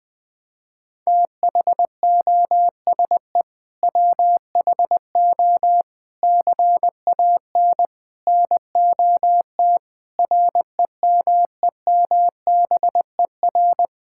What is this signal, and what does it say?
Morse code, heard as a keyed tone, about 20 wpm, 700 Hz. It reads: THOSE WHO CAN NOT REMEMBER
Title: Sets of 8 Words - 20wpm